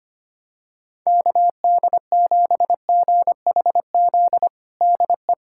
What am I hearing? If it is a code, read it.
Morse code, heard as a keyed tone, about 25 wpm, 700 Hz. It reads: KD7GHZ DE
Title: USA Callsign DE Callsign - 6 Characters - 25wpm